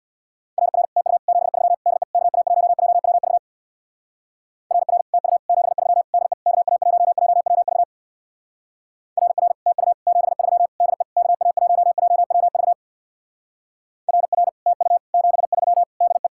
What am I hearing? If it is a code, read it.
Morse code, heard as a keyed tone, about 67 wpm, 700 Hz. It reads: RR TU 73 DE KA1JWV RR TU 73 DE KA1JWV RR TU 73 DE KA1JWV RR TU 73 DE